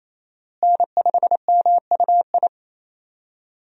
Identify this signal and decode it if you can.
Morse code, heard as a keyed tone, about 28 wpm, 700 Hz. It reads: N5MUI
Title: USA Call Signs - 5 Characters - Repeated 2x - 28wpm